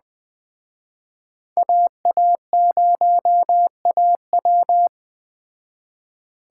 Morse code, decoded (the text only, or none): AA0AW